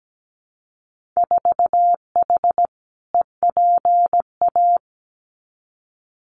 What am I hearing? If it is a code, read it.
Morse code, heard as a keyed tone, about 17 wpm, 700 Hz. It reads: 4H EPA
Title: Winter Field Day Exchanges 17wpm